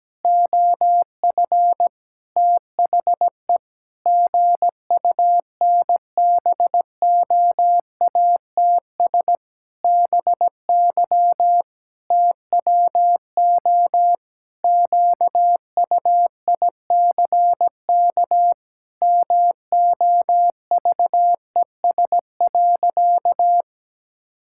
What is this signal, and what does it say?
Morse code, heard as a keyed tone, about 17 wpm, 700 Hz. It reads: OF THE GUNBOATS BY TWO QUICK MOVES.
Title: Pangram Challenge v2 - 17wpm